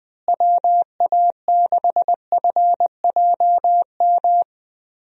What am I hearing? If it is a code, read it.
Morse code, heard as a keyed tone, about 20 wpm, 700 Hz. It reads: WA6FJM
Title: USA Call Signs - 6 Characters - Repeated 2x - 20wpm